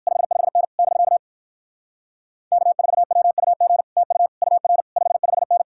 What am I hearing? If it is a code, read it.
Morse code, heard as a keyed tone, about 59 wpm, 700 Hz. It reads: 55N <BK> K4WUG TU UR 55N